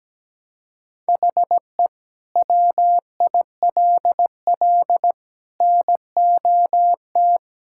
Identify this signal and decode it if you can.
Morse code, heard as a keyed tone, about 17 wpm, 700 Hz. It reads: HE WILL NOT
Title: Sentences from Top 1600 Words 17wpm